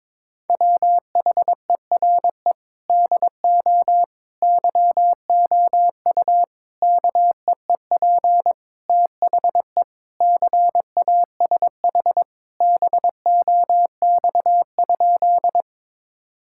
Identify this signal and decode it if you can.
Morse code, heard as a keyed tone, about 21 wpm, 700 Hz. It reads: WHERE DO YOU KEEP THE CASH BOX?
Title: Sentences from Top 1000 Words - Review Part 5 - 22wpm